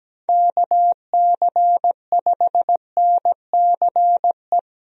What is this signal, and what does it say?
Morse code, heard as a keyed tone, about 17 wpm, 700 Hz. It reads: KC5NCE
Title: USA Call Signs - 6 Characters - 17wpm